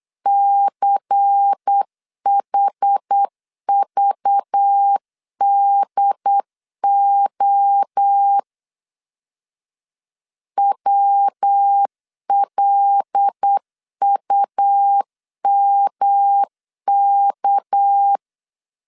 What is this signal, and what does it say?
Morse code, heard as a keyed tone, about 8 wpm, 800 Hz. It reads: CHVDO WLUMK
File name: LEZIONI / 6 PKX0 / 6 PK 8wpm 3003